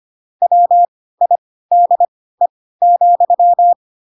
Morse code, heard as a keyed tone, about 25 wpm, spaced as for 16 wpm, 700 Hz. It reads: WIDE,
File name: Pinocchio / Pinocchio ch1 10 25 ik1hge